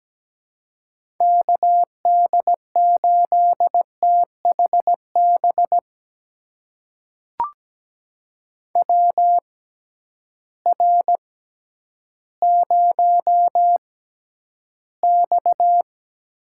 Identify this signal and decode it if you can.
Morse code, heard as a keyed tone, about 17 wpm, spaced as for 10 wpm, 700 Hz. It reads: KD8THB WR0X
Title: USA Call Signs - 6x Character Spacing - 17wpm